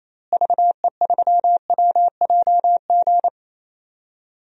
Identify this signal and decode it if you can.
Morse code, heard as a keyed tone, about 28 wpm, 700 Hz. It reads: VE3WJG
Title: DXCC-100 Call Signs - 6 Characters - Repeated 2x - 28wpm